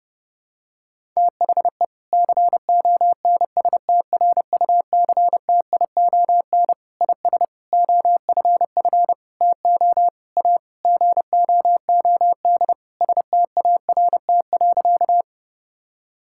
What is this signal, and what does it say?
Morse code, heard as a keyed tone, about 30 wpm, 700 Hz. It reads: THE CONSTRUCTION IS OFF TO A GOOD START.
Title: Sentences from Top 1600 Words 30wpm